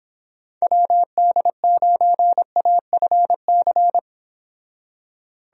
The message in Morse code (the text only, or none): WD9AFC